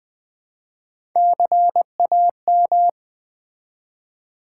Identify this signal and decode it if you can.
Morse code, heard as a keyed tone, about 20 wpm, 700 Hz. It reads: CAM
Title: Build A Word - 1 Letter At A Time - 4-Letter Words - 20wpm